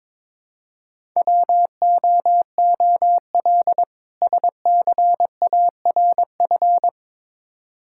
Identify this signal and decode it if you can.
Morse code, heard as a keyed tone, about 22 wpm, 700 Hz. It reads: WOOL SCARF
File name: Difficult Words with Hints - 22wpm